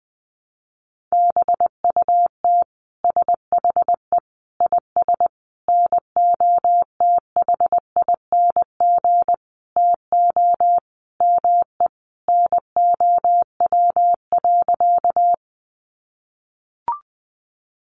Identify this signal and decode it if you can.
Morse code, heard as a keyed tone, about 20 wpm, 700 Hz. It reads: BUT SHE IS NOTHING TO ME NOW.
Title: Sentences from Top 500 Words 20wpm